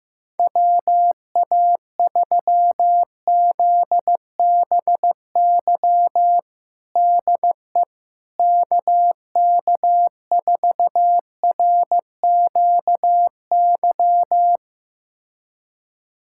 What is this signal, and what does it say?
Morse code, heard as a keyed tone, about 15 wpm, 700 Hz. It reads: WA3ZBY DE KK4RQY
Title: USA Callsign DE Callsign 15wpm